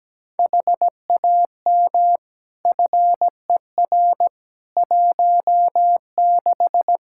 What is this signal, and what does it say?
Morse code, heard as a keyed tone, about 17 wpm, 700 Hz. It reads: HAM FER 16
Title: Practice QSO - Exchange 3 - More Info - 17wpm